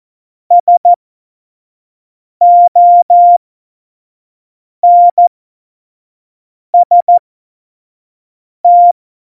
Morse code, heard as a keyed tone, about 14 wpm, spaced as for 5 wpm, 700 Hz. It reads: SONST
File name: Morsekurs / Witze / Witz 01. Oktober 2019, 25 F70